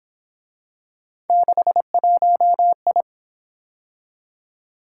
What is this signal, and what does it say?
Morse code, heard as a keyed tone, about 26 wpm, 700 Hz. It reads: 61I
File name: Speed-Racing - Three Letter-Number Combinations - 22wpm